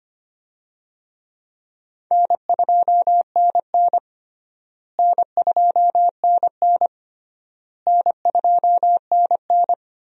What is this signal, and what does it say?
Morse code, heard as a keyed tone, about 25 wpm, 700 Hz. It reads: N2NN N2NN N2NN
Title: USA Call Signs - 4 Characters - Repeated 3x - 25wpm